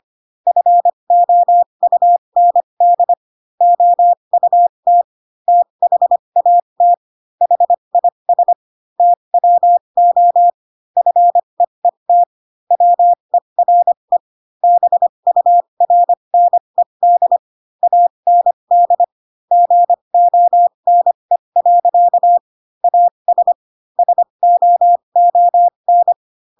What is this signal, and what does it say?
Morse code, heard as a keyed tone, about 25 wpm, spaced as for 22 wpm, 700 Hz. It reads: FOUND OUT THAT HIS TWO FEET WERE BURNED AND GONE. AS SOON
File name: Pinocchio / Pinocchio ch7 18 25 ik1hge